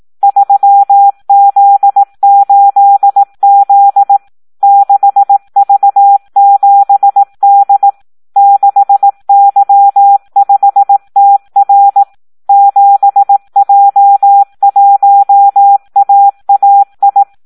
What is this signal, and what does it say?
Morse code, heard as a keyed tone, about 18 wpm, 800 Hz. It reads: 3Z8Z 6V7D 6Y5TR 7J1AAI
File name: Audio practica CW / SD 1 18